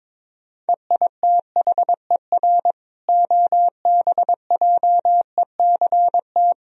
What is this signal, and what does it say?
Morse code, heard as a keyed tone, about 22 wpm, 700 Hz. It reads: EITHER OBJECT
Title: Sentences from Top 1000 Words - Review Part 4 - 22wpm